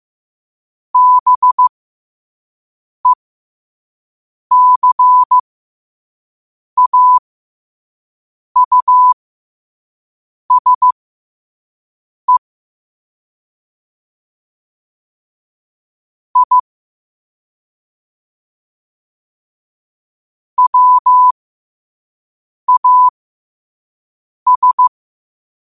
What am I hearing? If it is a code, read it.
Morse code, heard as a keyed tone, about 15 wpm, spaced as for 5 wpm, 1000 Hz. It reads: BECAUSE I WAS
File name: qotdfiles / 5wpm